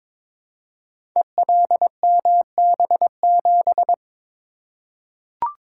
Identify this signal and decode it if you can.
Morse code, heard as a keyed tone, about 22 wpm, 700 Hz. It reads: ELMB7
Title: Code Groups - 5 Letter:Numbers - 22wpm